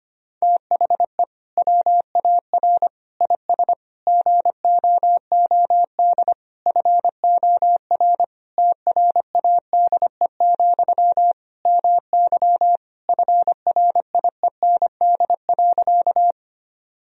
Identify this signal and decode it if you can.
Morse code, heard as a keyed tone, about 25 wpm, 700 Hz. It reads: THE WAR IS GOOD FOR TRADE, MY FRIEND.
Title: Sentences from Top 1000 Words - Review Part 4 - 25wpm